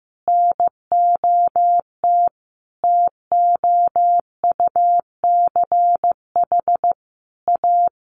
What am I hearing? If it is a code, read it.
Morse code, heard as a keyed tone, about 15 wpm, 700 Hz. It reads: NOT TOUCH A
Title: Sentences from Top 1500 Words 15wpm